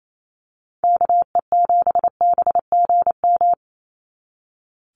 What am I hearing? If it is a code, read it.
Morse code, heard as a keyed tone, about 28 wpm, 700 Hz. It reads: KE7BGM
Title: NA SOTA Activator Callsigns 28wpm